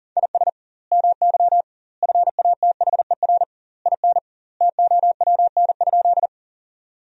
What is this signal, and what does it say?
Morse code, heard as a keyed tone, about 40 wpm, 700 Hz. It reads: IS MY FATHER IN TOWN?
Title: 2x Word-Spacing - Sentences from Top 500 Words - 40wpm